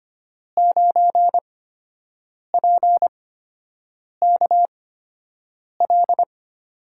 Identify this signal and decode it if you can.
Morse code, heard as a keyed tone, about 25 wpm, spaced as for 7 wpm, 700 Hz. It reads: 9PKL
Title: USA Call Signs - 5 Characters - 8x Character Spacing - 25wpm